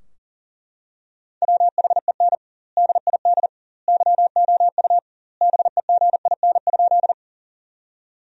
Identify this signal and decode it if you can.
Morse code, heard as a keyed tone, about 40 wpm, 700 Hz. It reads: WHEN DID YOU BEGIN?
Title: 2x Word-Spacing - Sentences from Top 300 Words - 40wpm